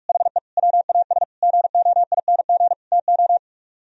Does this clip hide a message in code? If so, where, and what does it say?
Yes: Morse code from 0.1 to 3.4 s, heard as a keyed tone, about 50 wpm, 700 Hz. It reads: HE WAS GOING TO